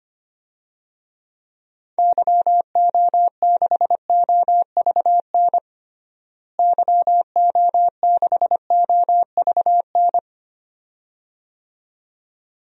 Morse code, heard as a keyed tone, about 25 wpm, 700 Hz. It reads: YO6OVN YO6OVN
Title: DXCC-100 Call Signs - 6 Characters - Repeated 2x - 25wpm